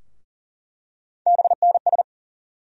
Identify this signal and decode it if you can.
Morse code, heard as a keyed tone, about 40 wpm, 700 Hz. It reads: BNS
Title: USA Call Sign Postfixes - 3 Letters 40wpm